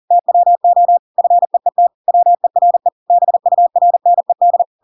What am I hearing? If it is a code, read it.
Morse code, heard as a keyed tone, about 40 wpm, 700 Hz. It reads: TWO FEET WERE BURNED
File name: Pinocchio / Pinocchio ch7 40 40 ik1hge